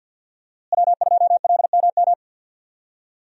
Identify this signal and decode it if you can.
Morse code, heard as a keyed tone, about 49 wpm, 700 Hz. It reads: W2LMM
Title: USA Call Signs - 5 Characters - Repeated 2x - 50wpm